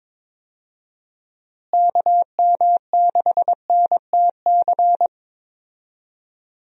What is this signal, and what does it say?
Morse code, heard as a keyed tone, about 22 wpm, 700 Hz. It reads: KM6NTC
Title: USA Call Signs - 6 Characters - Repeated 3x - 22wpm